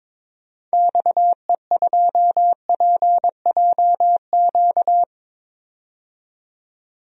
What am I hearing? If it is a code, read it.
Morse code, heard as a keyed tone, about 21 wpm, 700 Hz. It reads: XE2PJQ
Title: DXCC-100 Call Signs - 6 Characters - 22wpm